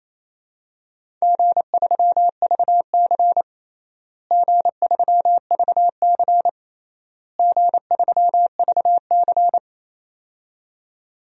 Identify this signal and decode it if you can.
Morse code, heard as a keyed tone, about 28 wpm, 700 Hz. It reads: G3VC G3VC G3VC
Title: DXCC-100 Call Signs - 4 Characters - Repeated 3x - 28wpm